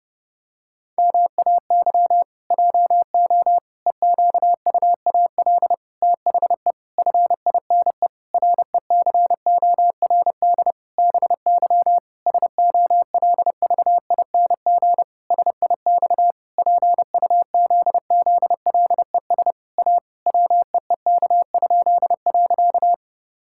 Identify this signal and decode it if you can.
Morse code, heard as a keyed tone, about 32 wpm, 700 Hz. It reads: MAY JO EQUAL THE FINE RECORD BY SOLVING SIX PUZZLES A WEEK?.
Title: Pangram Challenge v2 - 30wpm